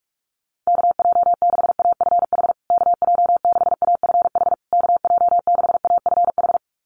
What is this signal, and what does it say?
Morse code, heard as a keyed tone, about 46 wpm, 700 Hz. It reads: KJ6AFH KJ6AFH KJ6AFH
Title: USA Call Signs Repeated 3x 45wpm